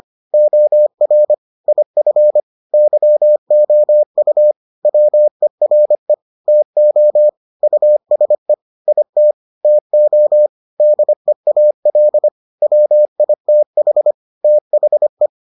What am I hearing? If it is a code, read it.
Morse code, heard as a keyed tone, about 25 wpm, 600 Hz. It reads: OR IF YOU WERE TO USE IT TO DEAL WITH THE